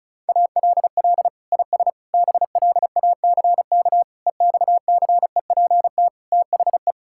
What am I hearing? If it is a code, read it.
Morse code, heard as a keyed tone, about 36 wpm, 700 Hz. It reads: ALL IS BLACK EXCEPT THE
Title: Sentences from Top 1600 Words 35wpm